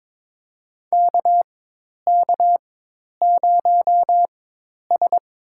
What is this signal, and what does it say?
Morse code, heard as a keyed tone, about 22 wpm, spaced as for 10 wpm, 700 Hz. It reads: KK0S
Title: USA Call Signs - 4 Characters - 4x Character Spacing - 22wpm